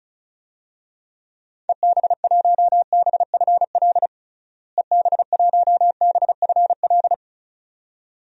Morse code, heard as a keyed tone, about 34 wpm, 700 Hz. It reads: EB1BFL EB1BFL
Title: DXCC-100 Call Signs - 6 Characters - Repeated 2x - 35wpm